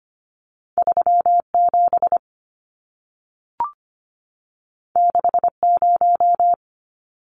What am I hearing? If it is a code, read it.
Morse code, heard as a keyed tone, about 25 wpm, 700 Hz. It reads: 37 60
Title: Numbers - 2-Digit 25wpm